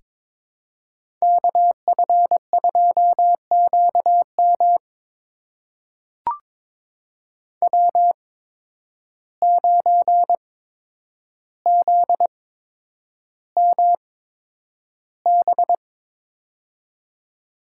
Morse code, heard as a keyed tone, about 22 wpm, spaced as for 9 wpm, 700 Hz. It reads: KF2QM W9ZMB
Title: USA Call Signs - 5 Characters - 8x Character Spacing - 22wpm